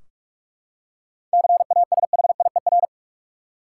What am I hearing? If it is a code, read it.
Morse code, heard as a keyed tone, about 46 wpm, 700 Hz. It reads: CASHIER
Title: Practice QSO - Exchange 3 - More Info - 45wpm